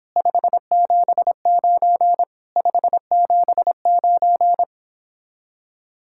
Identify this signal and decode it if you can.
Morse code, heard as a keyed tone, about 26 wpm, 700 Hz. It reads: 579 579